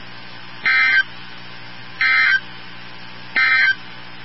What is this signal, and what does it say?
Emergency Alert System (SAME) tones, heard as harsh buzzing bursts. It is heard from 0.5 to 3.8 s.
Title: NOAA Weather Radio 162.550